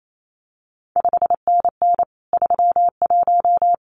Morse code, heard as a keyed tone, about 28 wpm, 700 Hz. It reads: 5NN 31
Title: CQ WW DX Contest Exchange - 28wpm